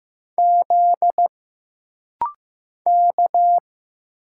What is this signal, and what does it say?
Morse code, heard as a keyed tone, about 15 wpm, 700 Hz. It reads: Z K
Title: Single Letters - Rapid-Fire 15wpm